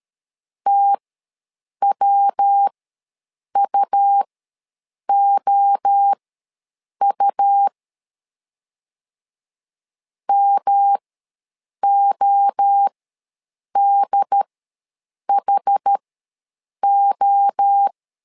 Morse code, heard as a keyed tone, about 13 wpm, spaced as for 7 wpm, 800 Hz. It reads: TWUOU MODHO
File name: LEZIONI / 3 MWRO / 3 MWRO ws 12wpm 1914